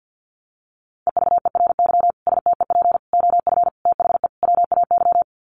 Morse code, heard as a keyed tone, about 50 wpm, 700 Hz. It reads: EVERY STEP OF THE WAY
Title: Sets of 5 Words Encore2 50wpm